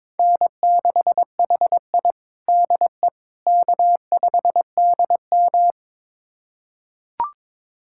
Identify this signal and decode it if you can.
Morse code, heard as a keyed tone, about 22 wpm, 700 Hz. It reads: N6HI DE K5DM
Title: USA Callsign DE Callsign - 4 Characters - 22wpm